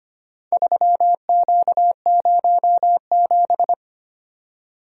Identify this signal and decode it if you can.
Morse code, heard as a keyed tone, about 25 wpm, 700 Hz. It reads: 3Q07
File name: practice / 25wpm / Four Letter-Number Combinations - 25wpm